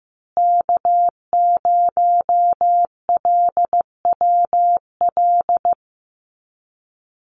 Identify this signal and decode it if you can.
Morse code, heard as a keyed tone, about 15 wpm, 700 Hz. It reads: K0LWL
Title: USA Call Signs Repeated 3x 15wpm